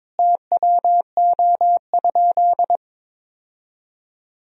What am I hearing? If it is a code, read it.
Morse code, heard as a keyed tone, about 22 wpm, 700 Hz. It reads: TWO?
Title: Sentences from Top 1000 Words - Review Part 1 - 22wpm